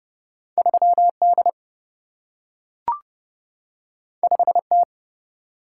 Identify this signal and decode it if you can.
Morse code, heard as a keyed tone, about 30 wpm, 700 Hz. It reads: 3D 5T
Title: Single-Letter Single-Number 30wpm